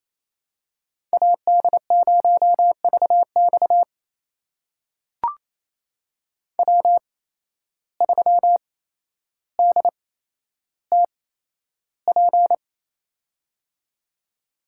Morse code, heard as a keyed tone, about 28 wpm, spaced as for 12 wpm, 700 Hz. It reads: AD0VX W3DTP
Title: USA Call Signs - 5 Characters - 8x Character Spacing - 28wpm